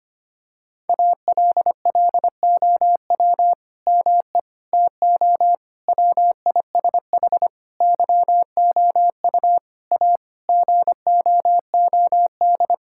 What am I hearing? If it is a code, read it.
Morse code, heard as a keyed tone, about 25 wpm, 700 Hz. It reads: ALLOW ME TO WISH YOU A GOOD
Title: Sentences from Top 1600 Words 25wpm